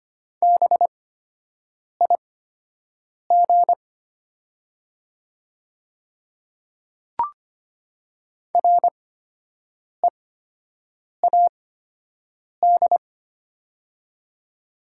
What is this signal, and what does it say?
Morse code, heard as a keyed tone, about 25 wpm, spaced as for 7 wpm, 700 Hz. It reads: BIG READ
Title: 8x Character Spacing - Top 300 Words - 25wpm